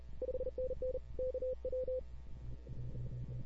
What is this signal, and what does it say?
Morse code, heard as a keyed tone, about 37 wpm, 500 Hz. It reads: 5NN KW
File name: CW / 2005-k5tr arrl cw dx -00:00